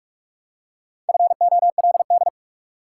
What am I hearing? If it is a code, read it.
Morse code, heard as a keyed tone, about 46 wpm, 700 Hz. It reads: FOLD
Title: Four Letter Words - 45wpm